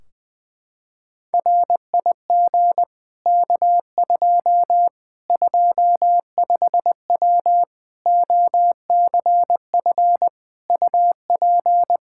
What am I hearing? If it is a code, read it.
Morse code, heard as a keyed tone, about 20 wpm, 700 Hz. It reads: RIG K2 25W OCF UP
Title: Practice QSO - Exchange 2 - Info - 20wpm